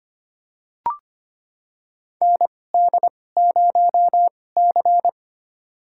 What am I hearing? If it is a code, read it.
Morse code, heard as a keyed tone, about 25 wpm, spaced as for 18 wpm, 700 Hz. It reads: ND0C
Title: USA Call Signs - 4 Characters - 2x Character Spacing - 25wpm